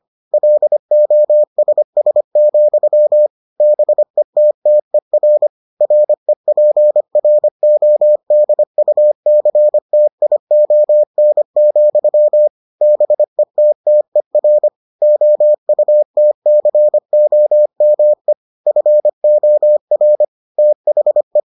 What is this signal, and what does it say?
Morse code, heard as a keyed tone, about 25 wpm, 600 Hz. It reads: LOSS, BETTER REPRODUCTION, BETTER OUTCOME FOR THE